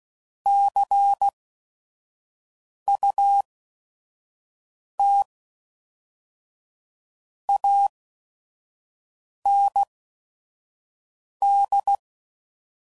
Morse code, heard as a keyed tone, about 16 wpm, spaced as for 5 wpm, 800 Hz. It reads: CUT AND